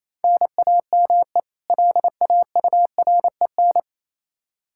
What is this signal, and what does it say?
Morse code, heard as a keyed tone, about 28 wpm, 700 Hz. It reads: NAME LAUREN
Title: K1USN SST QSO Practice 28wpm